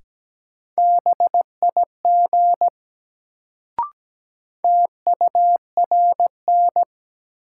Morse code, heard as a keyed tone, about 17 wpm, 700 Hz. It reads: BIG TURN
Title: Sentences from Top 200 Words - New Word Drill - 17wpm